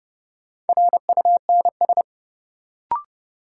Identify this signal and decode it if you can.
Morse code, heard as a keyed tone, about 31 wpm, 700 Hz. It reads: RUNS
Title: Sentences from Top 600 Words - New Word Drill - 30wpm